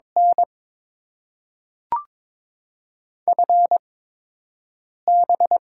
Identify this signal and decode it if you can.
Morse code, heard as a keyed tone, about 22 wpm, spaced as for 6 wpm, 700 Hz. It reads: N FB
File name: Top 20 QSO Elements - 8x Character Spacing - 22wpm